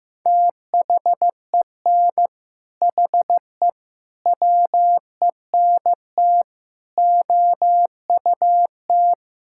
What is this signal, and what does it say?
Morse code, heard as a keyed tone, about 15 wpm, 700 Hz. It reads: THEN HE WENT OUT
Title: Sentences from Top 1000 Words - Review Part 5 - 15wpm